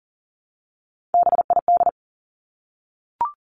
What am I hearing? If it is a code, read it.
Morse code, heard as a keyed tone, about 42 wpm, 700 Hz. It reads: BID
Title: Three Challenging Letter-Number Combinations 40wpm